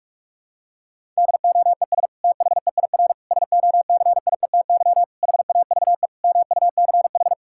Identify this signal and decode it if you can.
Morse code, heard as a keyed tone, about 46 wpm, 700 Hz. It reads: DOES THEIR SOCIETY HAVE MUCH
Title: Sentences from Top 1000 Words - Review Part 6 - 45wpm